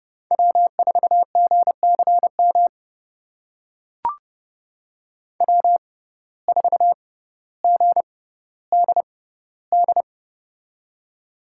Morse code, heard as a keyed tone, about 32 wpm, spaced as for 16 wpm, 700 Hz. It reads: W4GCM W4GDD
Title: USA Call Signs - 5 Characters - 6x Character Spacing - 30wpm